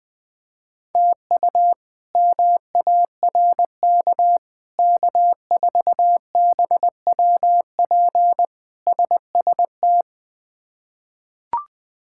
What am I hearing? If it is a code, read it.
Morse code, heard as a keyed tone, about 20 wpm, 700 Hz. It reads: TU MARK K4BWP SST
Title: K1USN SST QSO Practice - 20wpm